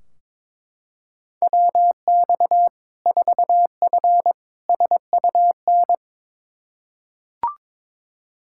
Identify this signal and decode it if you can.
Morse code, heard as a keyed tone, about 22 wpm, 700 Hz. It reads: WX 4F SUN
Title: Practice QSO - Exchange 2 - Info - 22wpm